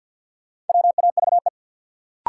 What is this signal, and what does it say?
Morse code, heard as a keyed tone, about 47 wpm, 700 Hz. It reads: WAVE
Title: Sentences from Top 1400 Words - New Word Drill Alt - 50wpm